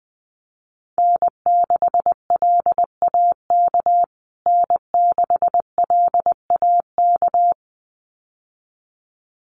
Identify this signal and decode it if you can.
Morse code, heard as a keyed tone, about 20 wpm, 700 Hz. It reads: N6LAK N6LAK
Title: USA Call Signs Repeated 2x 20wpm